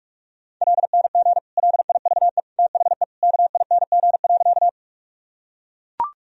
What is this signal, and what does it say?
Morse code, heard as a keyed tone, about 42 wpm, 700 Hz. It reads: LNG LIVE THE KING.
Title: Sentences from Top 400 Words and Abbreviations 45wpm